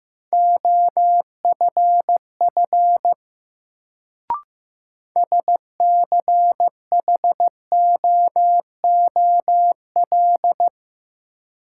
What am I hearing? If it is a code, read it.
Morse code, heard as a keyed tone, about 15 wpm, 700 Hz. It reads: OFF SCHOOL
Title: Sets of 2 Words - New Word Drill - 15wpm